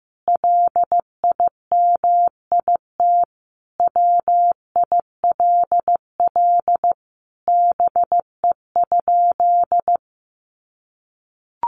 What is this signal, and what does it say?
Morse code, heard as a keyed tone, about 15 wpm, 700 Hz. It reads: LIMIT WILL BE?
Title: Sentences from Top 1500 Words 15wpm